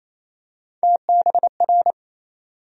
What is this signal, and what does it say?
Morse code, heard as a keyed tone, about 27 wpm, 700 Hz. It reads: TBR
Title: USA Call Sign Postfixes - 3 Letters 28wpm